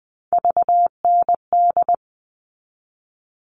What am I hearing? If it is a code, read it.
Morse code, heard as a keyed tone, about 20 wpm, 700 Hz. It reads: VND
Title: All Cut Numbers - 3-Digit 20wpm